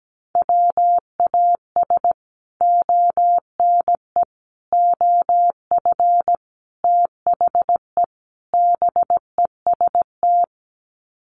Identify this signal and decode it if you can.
Morse code, heard as a keyed tone, about 17 wpm, 700 Hz. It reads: WAS ONE OF THE BEST
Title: Sets of 5 Words Encore2 17wpm